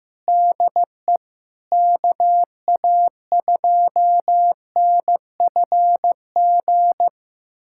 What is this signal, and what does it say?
Morse code, heard as a keyed tone, about 15 wpm, 700 Hz. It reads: DE KA2NFG
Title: USA Callsign DE Callsign - 6 Characters - 15wpm